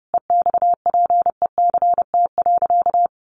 Morse code, heard as a keyed tone, about 30 wpm, 700 Hz. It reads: EXPECT.
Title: Sentences from Top 1200 Words 30wpm